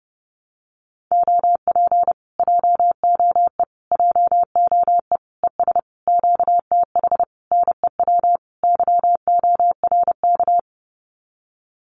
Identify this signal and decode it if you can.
Morse code, heard as a keyed tone, about 30 wpm, 700 Hz. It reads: OP JOE JOE ES QTH NEW YORK
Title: Common QSO Phrases Encore 30wpm